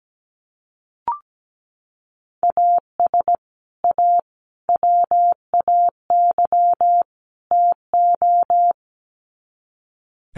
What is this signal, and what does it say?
Morse code, heard as a keyed tone, about 17 wpm, 700 Hz. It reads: AS A WAY TO
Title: Sets of 4 Words 17wpm